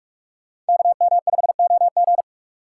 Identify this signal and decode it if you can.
Morse code, heard as a keyed tone, about 57 wpm, 700 Hz. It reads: KM5OG